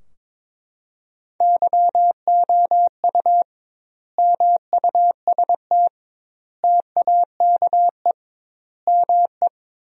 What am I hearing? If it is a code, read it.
Morse code, heard as a keyed tone, about 22 wpm, 700 Hz. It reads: YOU MUST TAKE ME
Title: 2x Word-Spacing - Sentences from Top 200 Words - 22wpm